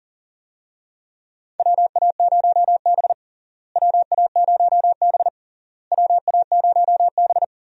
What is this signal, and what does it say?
Morse code, heard as a keyed tone, about 40 wpm, 700 Hz. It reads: WA0B WA0B WA0B
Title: USA Call Signs - 4 Characters - Repeated 3x - 40wpm